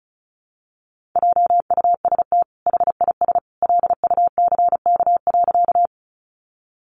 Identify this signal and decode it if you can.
Morse code, heard as a keyed tone, about 34 wpm, 700 Hz. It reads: JUST HIS LUCK.
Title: Sentences from Top 1500 Words 35wpm